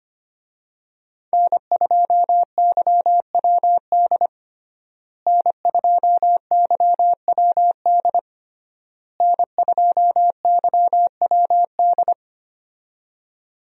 Morse code, heard as a keyed tone, about 25 wpm, 700 Hz. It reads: N2YWD N2YWD N2YWD
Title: USA Call Signs - 5 Characters - Repeated 3x - 25wpm